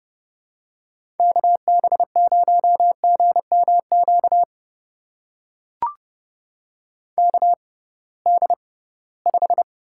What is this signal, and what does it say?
Morse code, heard as a keyed tone, about 30 wpm, spaced as for 19 wpm, 700 Hz. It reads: KB0GMQ KD5
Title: USA Call Signs - 6 Characters - 6x Character Spacing - 30wpm